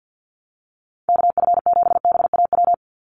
Morse code, heard as a keyed tone, about 50 wpm, 700 Hz. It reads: KF7BAW
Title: USA Call Signs 50wpm